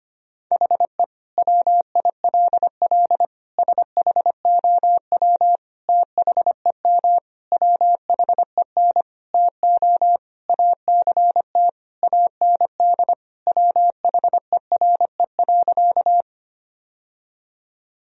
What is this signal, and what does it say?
Morse code, heard as a keyed tone, about 25 wpm, 700 Hz. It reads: HE WILL SHOW THEM WHEN TO ACT AND WHERE.
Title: Sentences from Top 1000 Words - Review Part 1 - 25wpm